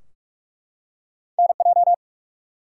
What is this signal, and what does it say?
Morse code, heard as a keyed tone, about 44 wpm, 700 Hz. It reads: NJ
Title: US States and Canadian Provinces 45wpm